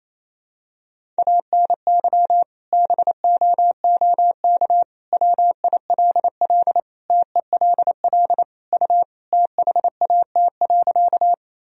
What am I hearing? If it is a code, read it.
Morse code, heard as a keyed tone, about 28 wpm, 700 Hz. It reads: ANY BOOK WILL TELL U THAT.
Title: Sentences from Top 300 Words and Abbreviations 28wpm